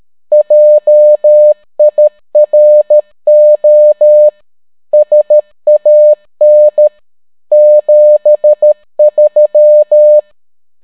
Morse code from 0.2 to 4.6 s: JIRO